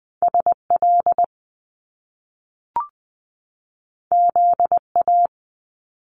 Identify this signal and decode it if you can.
Morse code, heard as a keyed tone, about 20 wpm, 700 Hz. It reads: SL ZA
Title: Top Two Letter Combinations in Words 20wpm